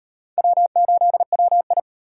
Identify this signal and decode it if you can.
Morse code, heard as a keyed tone, about 36 wpm, 700 Hz. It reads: W8WI